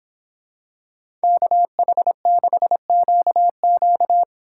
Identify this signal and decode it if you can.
Morse code, heard as a keyed tone, about 26 wpm, 700 Hz. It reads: KH6QQ